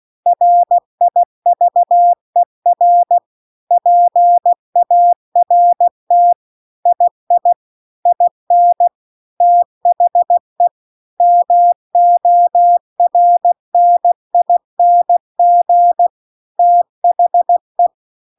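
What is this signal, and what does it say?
Morse code, heard as a keyed tone, about 16 wpm, 700 Hz. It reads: RIVER PART II IN THE MORNING THE
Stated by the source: Morse practice text at 16 WPM.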